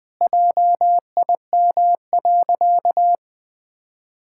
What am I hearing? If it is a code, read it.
Morse code, heard as a keyed tone, about 20 wpm, 700 Hz. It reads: JIM.
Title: Pangram Challenge v2 - 20wpm